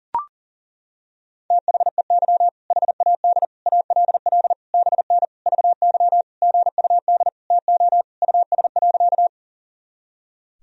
Morse code, heard as a keyed tone, about 37 wpm, 700 Hz. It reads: THEY HAD ALL BN VY GUD TO US.
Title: Sentences from Top 200 Words and Abbreviations 40wpm